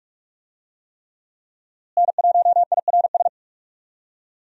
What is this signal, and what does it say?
Morse code, heard as a keyed tone, about 46 wpm, 700 Hz. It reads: N1IRS
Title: USA Call Signs - 5 Characters - Repeated 3x - 45wpm